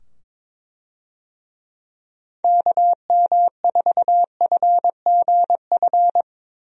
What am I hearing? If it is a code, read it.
Morse code, heard as a keyed tone, about 22 wpm, 700 Hz. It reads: KM4FGF
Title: USA Call Signs - 6 Characters - 22wpm